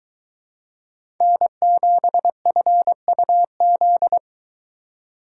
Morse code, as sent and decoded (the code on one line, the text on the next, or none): -. --... ..-. ..- --..
N7FUZ